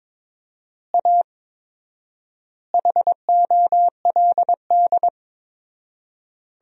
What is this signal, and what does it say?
Morse code, heard as a keyed tone, about 22 wpm, 700 Hz. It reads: A HOLD
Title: 4x Word-Spacing - Sentences from Top 400 Words - 22wpm